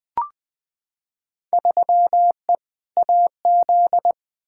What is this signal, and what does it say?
Morse code, heard as a keyed tone, about 20 wpm, 700 Hz. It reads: 3E AZ
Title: Field Day Exchanges Encore - 20wpm